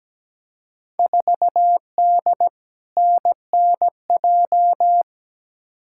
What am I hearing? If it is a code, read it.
Morse code, heard as a keyed tone, about 17 wpm, 700 Hz. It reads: 4D NNJ
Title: Field Day Exchanges Encore - 17wpm